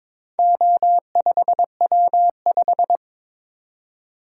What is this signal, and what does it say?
Morse code, heard as a keyed tone, about 21 wpm, 700 Hz. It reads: O5W5
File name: practice / 22wpm / Four Letter-Number Combinations - 22wpm